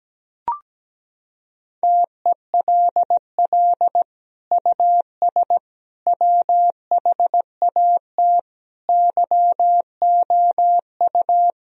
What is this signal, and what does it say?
Morse code, heard as a keyed tone, about 17 wpm, 700 Hz. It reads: TELL US WHAT YOU
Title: Sentences from Top 1000 Words - Review Part 4 - 17wpm